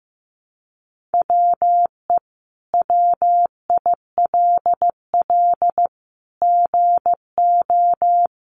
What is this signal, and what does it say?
Morse code, heard as a keyed tone, about 15 wpm, 700 Hz. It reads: WE WILL GO
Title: Sentences from Top 1300 Words 15wpm